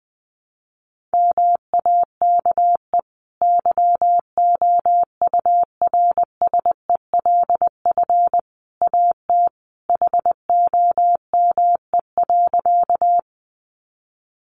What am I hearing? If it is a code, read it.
Morse code, heard as a keyed tone, about 20 wpm, 700 Hz. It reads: MAKE YOURSELF AT HOME.
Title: Sentences from Top 600 Words 20wpm